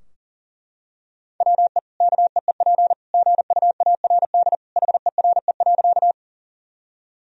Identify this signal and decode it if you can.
Morse code, heard as a keyed tone, about 40 wpm, 700 Hz. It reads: WE KEEP GUARD HERE.
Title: Sentences from Top 1000 Words - Review Part 4 - 40wpm